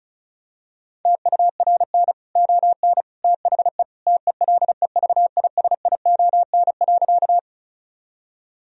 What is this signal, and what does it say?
Morse code, heard as a keyed tone, about 36 wpm, 700 Hz. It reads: TURN ON THE TELEVISION.
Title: Sentences from Top 1000 Words - Review Part 4 - 35wpm